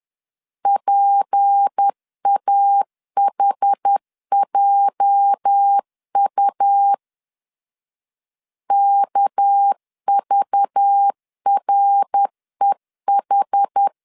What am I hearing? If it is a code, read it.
Morse code, heard as a keyed tone, about 10 wpm, 800 Hz. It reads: PAHJU KVREH
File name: LEZIONI / 8 YJ45 / 8 YJ 10wpm 2651